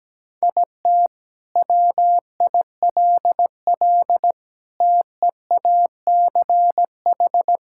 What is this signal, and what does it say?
Morse code, heard as a keyed tone, about 17 wpm, 700 Hz. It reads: IT WILL TEACH
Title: Sentences from Top 1600 Words 17wpm